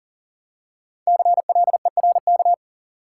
Morse code, heard as a keyed tone, about 40 wpm, 700 Hz. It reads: CLERK
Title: Five Letter Words - 40wpm